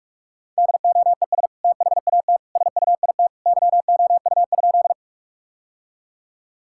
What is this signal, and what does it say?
Morse code, heard as a keyed tone, about 46 wpm, 700 Hz. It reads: DOES THAT SUIT YOU?
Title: Sentences from Top 1600 Words 45wpm